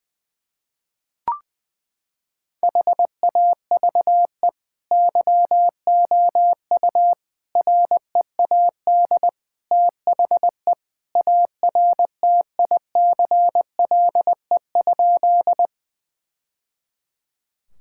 Morse code, heard as a keyed tone, about 20 wpm, 700 Hz. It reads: HAVE YOU READ THE ARTICLE?
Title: Sentences from Top 1000 Words - Review Part 5 - 20wpm